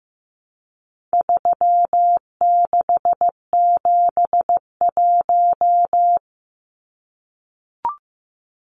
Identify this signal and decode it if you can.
Morse code, heard as a keyed tone, about 15 wpm, 700 Hz. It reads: 3671
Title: Numbers - 4-Digit 15wpm